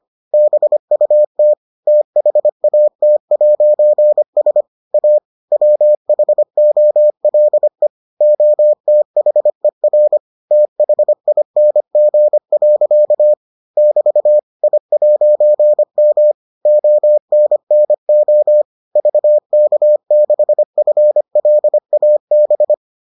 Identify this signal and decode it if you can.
Morse code, heard as a keyed tone, about 25 wpm, 600 Hz. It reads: BUT THAT'S A WHOLE OTHER THING. <BT> I'M ONNO VK6FLAB